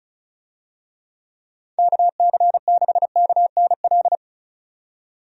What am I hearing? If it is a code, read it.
Morse code, heard as a keyed tone, about 34 wpm, 700 Hz. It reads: KC6KNL
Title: USA Call Signs - 6 Characters - 35wpm